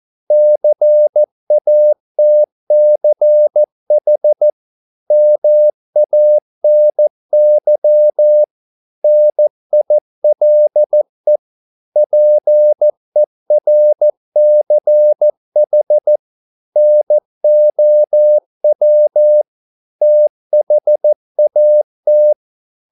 Morse code, heard as a keyed tone, about 14 wpm, 600 Hz. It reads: CATCH MANY NILE PERCH NOW THAT